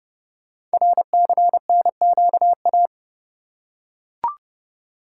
Morse code, heard as a keyed tone, about 28 wpm, 700 Hz. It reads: RCNQA
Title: Code Groups - 5 Letters - 30wpm